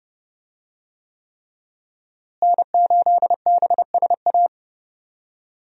Morse code, heard as a keyed tone, about 30 wpm, 700 Hz. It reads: N8BSA
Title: USA Call Signs - 5 Characters - Repeated 3x - 30wpm